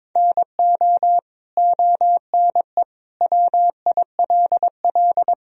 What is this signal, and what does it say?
Morse code, heard as a keyed tone, about 22 wpm, 700 Hz. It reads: NO ONE WILL
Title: Sentences from Top 100 Words and Abbreviations 22wpm